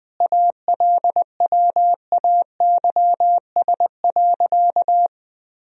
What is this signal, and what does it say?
Morse code, heard as a keyed tone, about 20 wpm, 700 Hz. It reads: ALWAYS.
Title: Sentences from Top 1000 Words - Review Part 4 - 20wpm